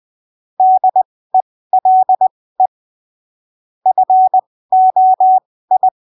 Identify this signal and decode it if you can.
Morse code, heard as a keyed tone, about 20 wpm, spaced as for 15 wpm, 750 Hz. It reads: DELE FOI
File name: arquivos / texto / texto 17 20ppm